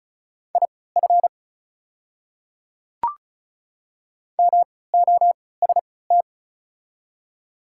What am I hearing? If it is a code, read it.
Morse code, heard as a keyed tone, about 36 wpm, spaced as for 20 wpm, 690 Hz. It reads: IF MOST
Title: 3x Character Spacing - Top 100 Words - 35wpm